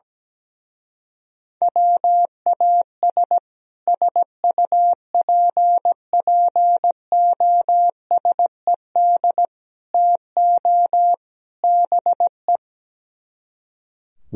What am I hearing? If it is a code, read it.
Morse code, heard as a keyed tone, about 17 wpm, 700 Hz. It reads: WAS SUPPOSED TO BE
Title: Speed-Racing - Sets of 4 Words 17wpm